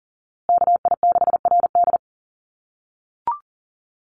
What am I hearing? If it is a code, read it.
Morse code, heard as a keyed tone, about 40 wpm, 700 Hz. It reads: KI6RD
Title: USA Call Signs Repeated 2x 40wpm